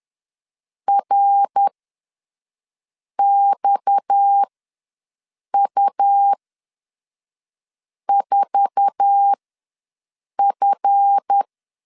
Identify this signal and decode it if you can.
Morse code, heard as a keyed tone, about 10 wpm, spaced as for 5 wpm, 800 Hz. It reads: RXU4F
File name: LEZIONI / 8 YJ45 / 8 YJ4 ws 10wpm 4020